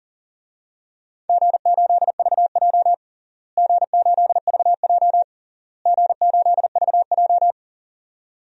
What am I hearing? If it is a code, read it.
Morse code, heard as a keyed tone, about 40 wpm, 700 Hz. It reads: G8VJ G8VJ G8VJ
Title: DXCC-100 Call Signs - 4 Characters - Repeated 3x - 40wpm